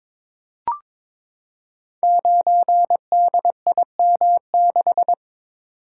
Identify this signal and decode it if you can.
Morse code, heard as a keyed tone, about 22 wpm, 700 Hz. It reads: 9DIM6
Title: Code Groups - 5 Letter:Numbers - 22wpm